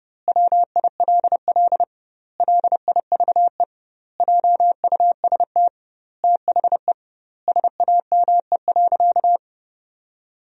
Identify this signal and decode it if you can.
Morse code, heard as a keyed tone, about 30 wpm, 700 Hz. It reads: WILL LIVE JUST THE SAME.
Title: 2x Word-Spacing - Sentences from Top 300 Words - 30wpm